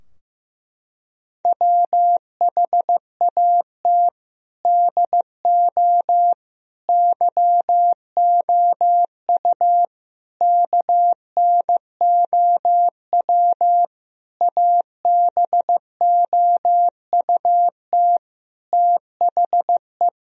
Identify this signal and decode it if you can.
Morse code, heard as a keyed tone, about 15 wpm, 700 Hz. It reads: WHAT DO YOU KNOW ABOUT THE
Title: Sentences from Top 1000 Words - Review Part 3 - 15wpm